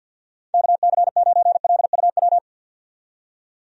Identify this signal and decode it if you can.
Morse code, heard as a keyed tone, about 47 wpm, 700 Hz. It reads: KC9LUW
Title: USA Call Signs - 6 Characters - 50wpm